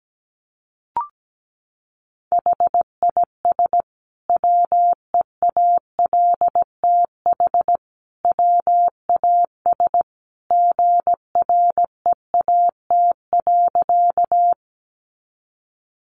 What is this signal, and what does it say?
Morse code, heard as a keyed tone, about 17 wpm, 700 Hz. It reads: HIS WEALTH WAS GREAT.
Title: Sentences from Top 1200 Words 17wpm